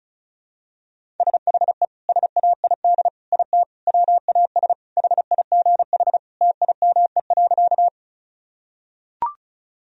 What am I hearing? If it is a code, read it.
Morse code, heard as a keyed tone, about 36 wpm, 700 Hz. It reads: SHE SAID IT WAS HIGH TIME.
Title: Sentences from Top 1000 Words - Review Part 1 - 35wpm